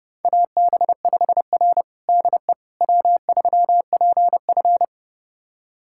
Morse code, heard as a keyed tone, about 30 wpm, 700 Hz. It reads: AB5R DE W3PF
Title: USA Callsign DE Callsign - 4 Characters - 30wpm